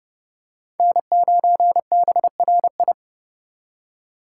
Morse code, heard as a keyed tone, about 29 wpm, 700 Hz. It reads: N9BRI